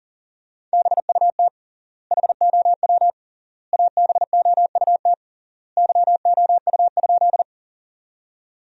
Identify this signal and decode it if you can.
Morse code, heard as a keyed tone, about 40 wpm, 700 Hz. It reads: BUT HOW ABOUT YOU?
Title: 3x Word-Spacing - Sentences from Top 100 Words - 40wpm